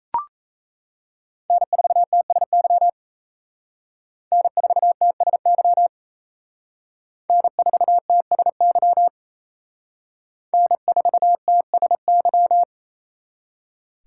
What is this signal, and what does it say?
Morse code, heard as a keyed tone, about 36 wpm, 700 Hz. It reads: N4TSY N4TSY N4TSY N4TSY